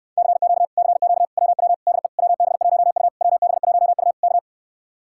Morse code, heard as a keyed tone, about 65 wpm, 700 Hz. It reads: CQ CQ WW DE WB1U WB1U K